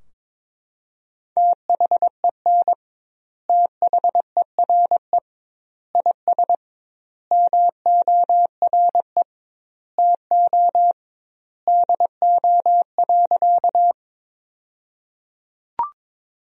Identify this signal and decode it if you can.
Morse code, heard as a keyed tone, about 22 wpm, 700 Hz. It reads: THEN THERE IS MORE TO DO.
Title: 2x Word-Spacing - Sentences from Top 100 Words - 22wpm